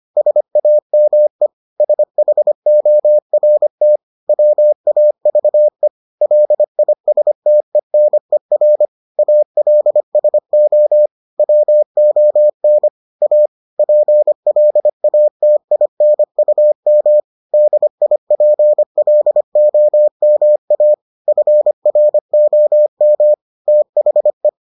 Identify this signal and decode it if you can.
Morse code, heard as a keyed tone, about 25 wpm, 600 Hz. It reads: SAME SHORT WAVE LISTENER ALSO WON A PLATINUM DIPLOMA FROM THE